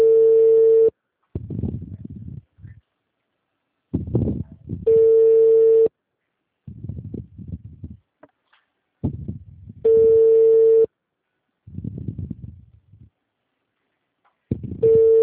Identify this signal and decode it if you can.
Ringback tone (China, 450 Hz).